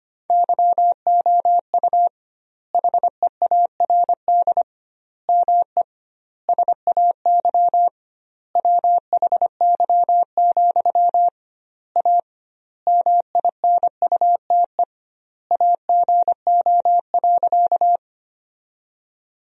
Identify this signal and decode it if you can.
Morse code, heard as a keyed tone, about 25 wpm, 700 Hz. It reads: YOU HEARD ME SAY WHY, A MINUTE AGO.
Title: 2x Word-Spacing - Sentences from Top 500 Words - 25wpm